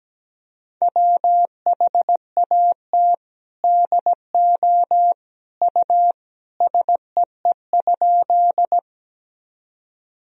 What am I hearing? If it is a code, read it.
Morse code, heard as a keyed tone, about 17 wpm, 700 Hz. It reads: WHAT DO U SEE?
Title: Sentences from Top 100 Words and Abbreviations 17wpm